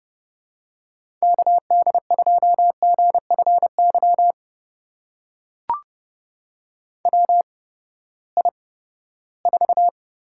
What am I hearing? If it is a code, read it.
Morse code, heard as a keyed tone, about 31 wpm, spaced as for 17 wpm, 700 Hz. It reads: KD2GFY WI4
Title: USA Call Signs - 8x Character Spacing - 30wpm